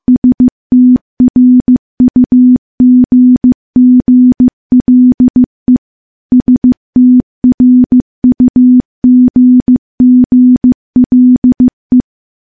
Morse code, heard as a keyed tone, about 15 wpm, 250 Hz. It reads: STRUGGLE STRUGGLE
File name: morse-training-15WPM